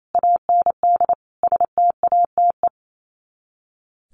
Morse code, heard as a keyed tone, about 28 wpm, 700 Hz. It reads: AND STATE
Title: Sets of 5 Words 28wpm